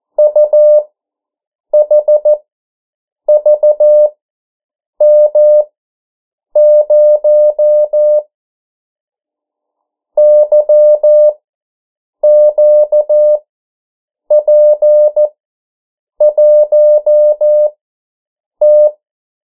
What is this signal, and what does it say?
Morse code, heard as a keyed tone, about 14 wpm, spaced as for 7 wpm, 600 Hz. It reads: UHVM0 YQP1T
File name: morse mp3 / 09 urok 8